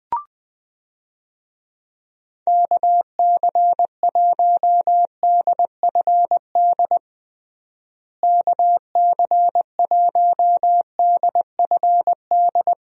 Morse code, heard as a keyed tone, about 21 wpm, 700 Hz. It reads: KC1DFD KC1DFD
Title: USA Call Signs - 6 Characters - Repeated 2x - 20wpm